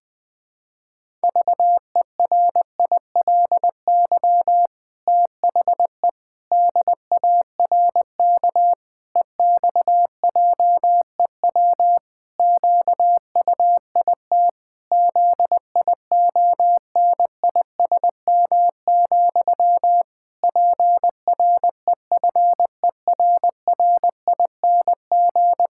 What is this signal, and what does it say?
Morse code, heard as a keyed tone, about 20 wpm, 700 Hz. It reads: VERILY THE DARK EXJEW QUIT ZIONISM, PREFERRING
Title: Pangram Challenge v2 - 20wpm